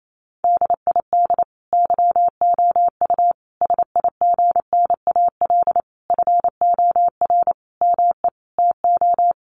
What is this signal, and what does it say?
Morse code, heard as a keyed tone, about 28 wpm, 700 Hz. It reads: DID YOU SIGNAL FOR ME TO
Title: Sentences from Top 1400 Words 28wpm